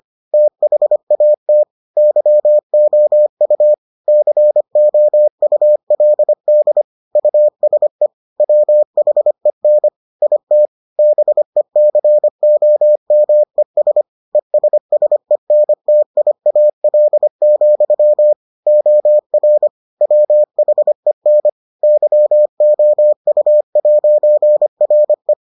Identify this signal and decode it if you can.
Morse code, heard as a keyed tone, about 25 wpm, 600 Hz. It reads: THAT YOU COULD USE WHEN IT BECOMES ESSENTIAL, OR WHEN YOU'RE